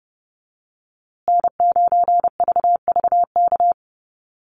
Morse code, heard as a keyed tone, about 30 wpm, 700 Hz. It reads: N9VVK
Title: USA Call Signs Repeated 2x 30wpm